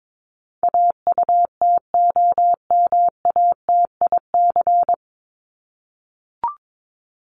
Morse code, heard as a keyed tone, about 22 wpm, 700 Hz. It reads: AUTOMATIC
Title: Top 1751-2000 Words 22wpm